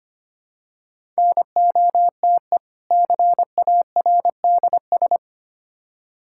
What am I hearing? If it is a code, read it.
Morse code, heard as a keyed tone, about 25 wpm, 700 Hz. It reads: NOTE CARDS
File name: practice / 25wpm / Difficult Words with Hints - 25wpm